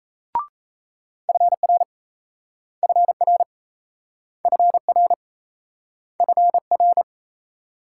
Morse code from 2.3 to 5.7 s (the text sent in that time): FR FR